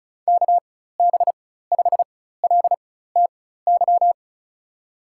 Morse code, heard as a keyed tone, about 35 wpm, spaced as for 16 wpm, 700 Hz. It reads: KB5LTY
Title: USA Call Signs - 4x Character Spacing - 35wpm